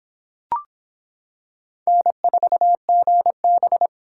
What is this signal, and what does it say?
Morse code, heard as a keyed tone, about 26 wpm, 700 Hz. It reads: N4GB